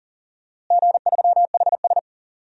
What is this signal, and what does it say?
Morse code, heard as a keyed tone, about 40 wpm, 700 Hz. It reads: G3HS
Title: DXCC-100 Call Signs - 4 Characters - Repeated 3x - 40wpm